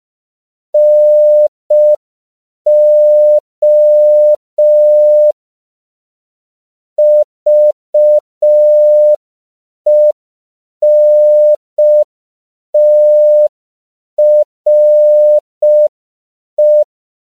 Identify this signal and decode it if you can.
Morse code, heard as a keyed tone, about 5 wpm, 600 Hz. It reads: NO VENTRE
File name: CW / 5PPM